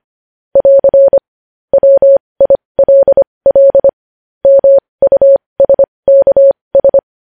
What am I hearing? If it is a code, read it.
Morse code, heard as a keyed tone, about 25 wpm, spaced as for 20 wpm, 550 Hz. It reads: <AR> WILL MUSKS
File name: podcast / News headlines-1x-25x20